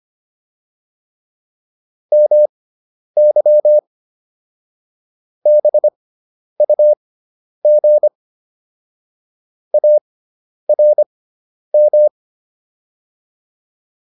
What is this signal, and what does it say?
Morse code, heard as a keyed tone, about 25 wpm, spaced as for 10 wpm, 600 Hz. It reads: MY BUG ARM